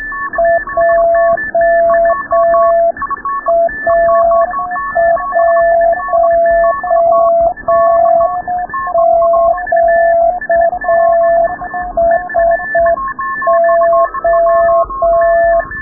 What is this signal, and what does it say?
Morse code, heard as a keyed tone, about 6 wpm, 690 Hz. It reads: JA1QSO
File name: rubberst01 30 noise